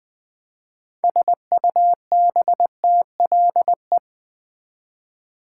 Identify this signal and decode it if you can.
Morse code, heard as a keyed tone, about 20 wpm, 700 Hz. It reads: SUBTLE
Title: 6 Letter Words - First 3 Letters Given - 20wpm